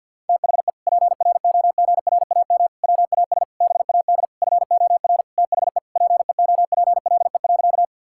Morse code, heard as a keyed tone, about 50 wpm, 700 Hz. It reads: THE PROGRAM WAS BAD FOR THE PEOPLE.
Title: Sentences from Top 1000 Words - Review Part 5 - 50wpm